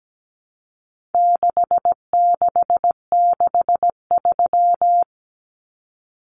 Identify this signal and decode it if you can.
Morse code, heard as a keyed tone, about 17 wpm, 700 Hz. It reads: 6663
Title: Numbers - 4-Digit 17wpm